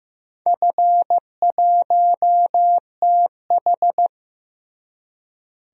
Morse code, heard as a keyed tone, about 15 wpm, 700 Hz. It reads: F1TH
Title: DXCC-100 Call Signs - 4 Characters - Repeated 3x - 15wpm